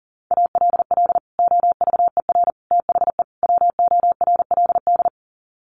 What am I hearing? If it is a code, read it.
Morse code, heard as a keyed tone, about 40 wpm, 700 Hz. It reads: ALL OVER THE WORLD
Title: Sets of 4 Words 40wpm